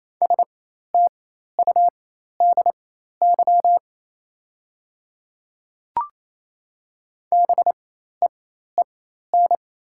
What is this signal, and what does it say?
Morse code, heard as a keyed tone, about 28 wpm, spaced as for 13 wpm, 700 Hz. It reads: STUDY BEEN
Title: 4x Character Spacing - Top 300 Words - 28wpm